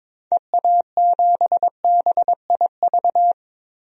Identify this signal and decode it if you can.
Morse code, heard as a keyed tone, about 22 wpm, 700 Hz. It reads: EA7BIV
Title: DXCC-100 Call Signs - 6 Characters - 22wpm